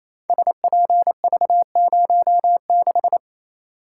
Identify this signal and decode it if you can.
Morse code, heard as a keyed tone, about 28 wpm, 700 Hz. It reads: SPV06
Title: Code Groups - 5 Letter:Numbers - 28wpm